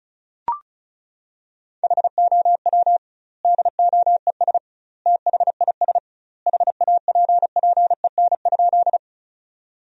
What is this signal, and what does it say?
Morse code, heard as a keyed tone, about 34 wpm, 700 Hz. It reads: HOW DOES THIS HAPPEN?
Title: 2x Word-Spacing - Sentences from Top 400 Words - 35wpm